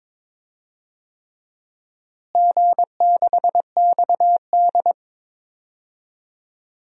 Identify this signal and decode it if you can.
Morse code, heard as a keyed tone, about 22 wpm, 700 Hz. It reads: G6XD
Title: DXCC-100 Call Signs - 4 Characters - Repeated 3x - 22wpm